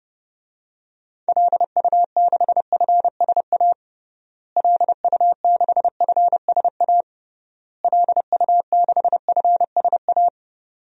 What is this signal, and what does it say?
Morse code, heard as a keyed tone, about 30 wpm, 700 Hz. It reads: LU6FSA LU6FSA LU6FSA
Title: DXCC-100 Call Signs - 6 Characters - Repeated 3x - 30wpm